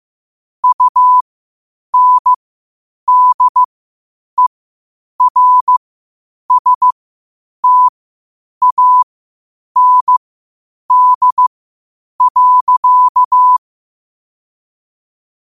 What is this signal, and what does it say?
Morse code, heard as a keyed tone, about 15 wpm, spaced as for 8 wpm, 1000 Hz. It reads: UNDERSTAND.
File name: qotdfiles / 8wpm